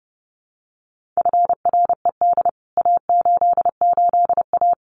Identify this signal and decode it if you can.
Morse code, heard as a keyed tone, about 30 wpm, 700 Hz. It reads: FRED A88A
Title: CWT Exchange 30wpm